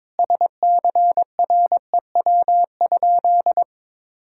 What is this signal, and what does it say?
Morse code, heard as a keyed tone, about 22 wpm, 700 Hz. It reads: SCREW?
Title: Sentences from Top 1600 Words 22wpm